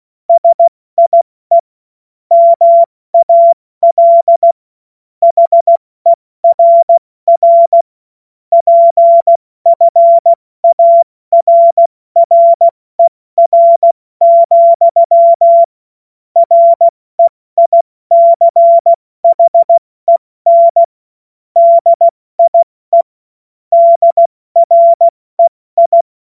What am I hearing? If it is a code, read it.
Morse code, heard as a keyed tone, about 16 wpm, 680 Hz. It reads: SIE MAL HERR PFARRER, REICHEN DIE DREI
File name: Morsekurs / Witze / Witz 09. Dezember 2025 70 F80